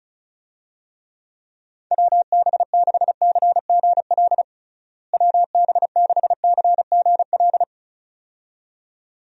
Morse code, heard as a keyed tone, about 36 wpm, 700 Hz. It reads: WB6CGL WB6CGL
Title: USA Call Signs - 6 Characters - Repeated 2x - 35wpm